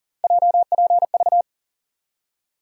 Morse code, heard as a keyed tone, about 40 wpm, 700 Hz. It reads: JPV
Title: USA Call Sign Postfixes - 3 Letters 40wpm